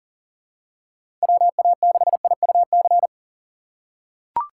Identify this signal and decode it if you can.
Morse code, heard as a keyed tone, about 40 wpm, 700 Hz. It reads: WA6IUC
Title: USA Call Signs - 2x Character Spacing - 40wpm